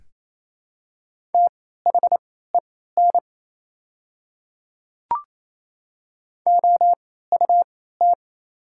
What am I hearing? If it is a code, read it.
Morse code, heard as a keyed tone, about 28 wpm, spaced as for 16 wpm, 700 Hz. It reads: THEN OUT
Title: 3x Character Spacing - Top 100 Words - 28wpm